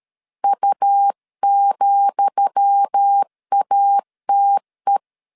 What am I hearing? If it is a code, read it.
Morse code, heard as a keyed tone, about 13 wpm, 800 Hz. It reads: U,ATE
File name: LEZIONI / 10 SIGNS / 10D COMA 12wpm 2544